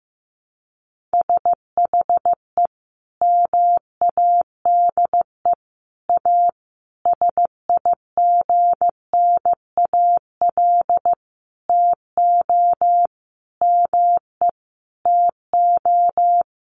Morse code, heard as a keyed tone, about 15 wpm, 700 Hz. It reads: SHE MADE A SIGNAL TO ME TO
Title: Sentences from Top 1400 Words 15wpm